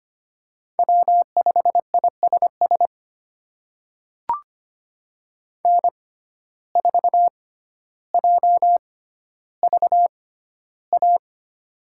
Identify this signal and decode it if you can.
Morse code, heard as a keyed tone, about 25 wpm, spaced as for 13 wpm, 700 Hz. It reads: W5ISS N4JVA
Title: USA Call Signs - 5 Characters - 6x Character Spacing - 25wpm